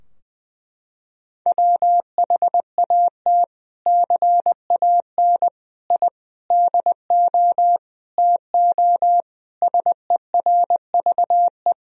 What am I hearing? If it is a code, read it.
Morse code, heard as a keyed tone, about 21 wpm, 700 Hz. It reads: WHAT CAN I DO TO SERVE
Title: Sentences from Top 1000 Words - Review Part 2 - 20wpm